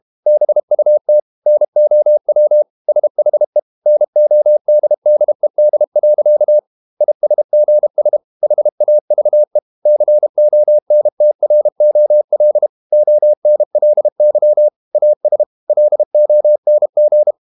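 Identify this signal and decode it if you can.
Morse code, heard as a keyed tone, about 32 wpm, 600 Hz. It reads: BUT NOW SHE NODDED. ISGS HAVE CONTROL ONLY AS LONG
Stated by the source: Morse code practice at 32 wpm from CW med Gnister, reading Ken Follett's novel Never in English.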